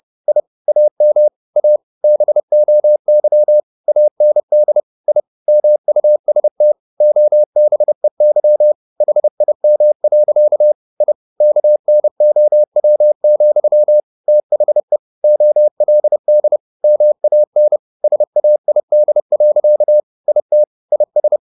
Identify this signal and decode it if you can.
Morse code, heard as a keyed tone, about 30 wpm, 600 Hz. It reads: I AM A BOY AND I MUST OBEY HIM. I KNOW, THE OLD MAN SAID. IT IS
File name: the old man and the sea / the old man and the sea 1 30wpm